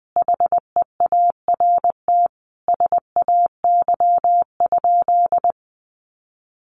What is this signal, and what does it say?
Morse code, heard as a keyed tone, about 20 wpm, 700 Hz. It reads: HEART SAY?
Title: Sentences from Top 1200 Words 20wpm